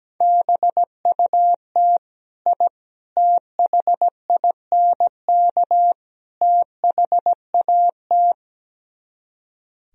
Morse code, heard as a keyed tone, about 17 wpm, 700 Hz. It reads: BUT I THINK THAT
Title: Speed-Racing - Sets of 4 Words 17wpm